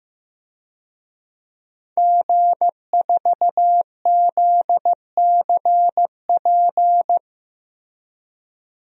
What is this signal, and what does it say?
Morse code, heard as a keyed tone, about 15 wpm, 700 Hz. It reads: G4ZCP
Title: DXCC-100 Call Signs - 5 Characters - Repeated 2x - 15wpm